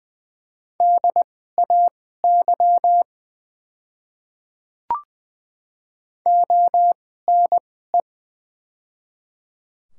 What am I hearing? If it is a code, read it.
Morse code, heard as a keyed tone, about 20 wpm, spaced as for 15 wpm, 700 Hz. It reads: DAY ONE
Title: 2x Character Spacing - Top 100 Words - 20wpm